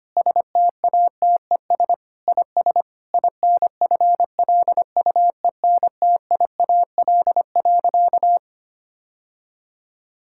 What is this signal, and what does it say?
Morse code, heard as a keyed tone, about 25 wpm, 700 Hz. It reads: STATES IS INFLUENTIAL.
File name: practice / 25wpm / Sentences with 4-Word Phrases - 25wpm